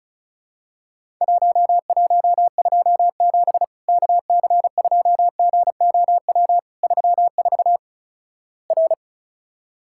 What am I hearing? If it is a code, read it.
Morse code, heard as a keyed tone, about 36 wpm, 700 Hz. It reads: 1127 KC2GOW 34 R
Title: Worked All Europe DX Contest - QTC Exchange - 35wpm